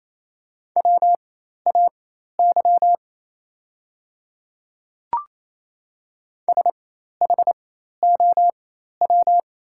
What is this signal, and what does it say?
Morse code, heard as a keyed tone, about 28 wpm, spaced as for 13 wpm, 700 Hz. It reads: WAY SHOW
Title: 4x Character Spacing - Top 200 Words - 28wpm